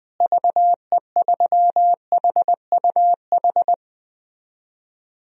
Morse code, heard as a keyed tone, about 20 wpm, 700 Hz. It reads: VE3HUH
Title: DXCC-100 Call Signs - 6 Characters - Repeated 3x - 20wpm